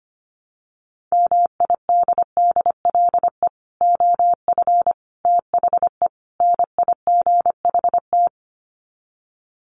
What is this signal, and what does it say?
Morse code, heard as a keyed tone, about 25 wpm, 700 Hz. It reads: MIDDLE OF THE NIGHT
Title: Sets of 4 Words 25wpm